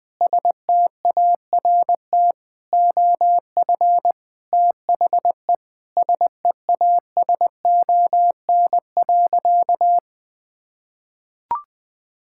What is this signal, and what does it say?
Morse code, heard as a keyed tone, about 21 wpm, 700 Hz. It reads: START OF THE SEASON.
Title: Sentences with 6-Word Phrases - 20wpm